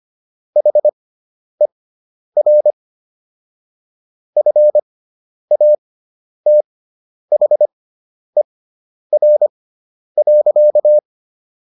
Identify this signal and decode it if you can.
Morse code, heard as a keyed tone, about 25 wpm, spaced as for 10 wpm, 600 Hz. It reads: HER FATHER.